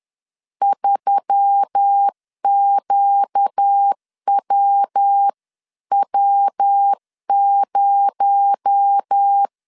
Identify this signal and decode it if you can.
Morse code, heard as a keyed tone, about 10 wpm, 800 Hz. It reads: 3QWW0
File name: CW / DET / 1 6 1500C S B 10wpm 4137